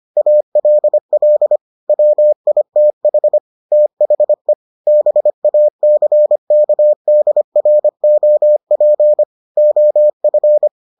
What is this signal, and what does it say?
Morse code, heard as a keyed tone, about 25 wpm, 600 Hz. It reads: ALL WITH THE BACKDROP OF